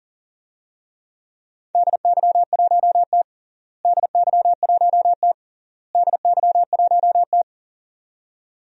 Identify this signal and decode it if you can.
Morse code, heard as a keyed tone, about 40 wpm, 700 Hz. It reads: DY1T DY1T DY1T
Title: DXCC-100 Call Signs - 4 Characters - Repeated 3x - 40wpm